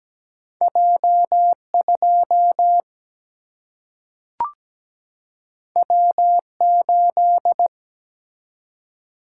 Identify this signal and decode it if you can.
Morse code, heard as a keyed tone, about 17 wpm, 700 Hz. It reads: J2 W8
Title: Single-Letter Single-Number 17wpm